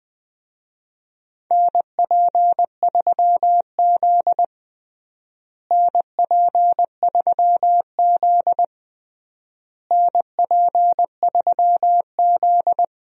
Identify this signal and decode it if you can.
Morse code, heard as a keyed tone, about 20 wpm, 700 Hz. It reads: NP3Z NP3Z NP3Z
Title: USA Call Signs - 4 Characters - Repeated 3x - 20wpm